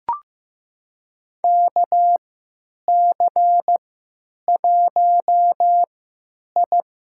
Morse code, heard as a keyed tone, about 15 wpm, spaced as for 9 wpm, 700 Hz. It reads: KC1I
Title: USA Call Signs - 3x Character Spacing - 15wpm